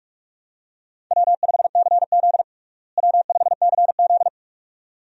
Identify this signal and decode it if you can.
Morse code, heard as a keyed tone, about 41 wpm, 700 Hz. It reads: W5CZ W5CZ
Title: USA Call Signs - 4 Characters - Repeated 2x - 45wpm